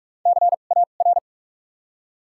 Morse code, heard as a keyed tone, about 43 wpm, spaced as for 32 wpm, 700 Hz. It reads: CAR
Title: 2x Character Spacing - Top 300 Words - 45wpm